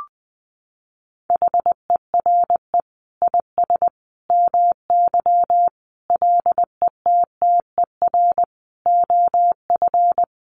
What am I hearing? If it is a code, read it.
Morse code, heard as a keyed tone, about 20 wpm, 700 Hz. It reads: HERE IS MY LETTER OF
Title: Sentences from Top 1400 Words 20wpm